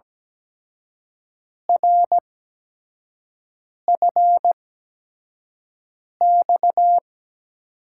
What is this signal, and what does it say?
Morse code, heard as a keyed tone, about 17 wpm, spaced as for 5 wpm, 700 Hz. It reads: RFX
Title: USA Call Signs - 5 Characters - 8x Character Spacing - 17wpm